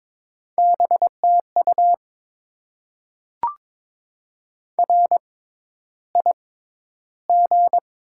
Morse code, heard as a keyed tone, about 22 wpm, spaced as for 11 wpm, 690 Hz. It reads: BTU RIG
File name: Top 20 QSO Elements - 6x Character Spacing - 22wpm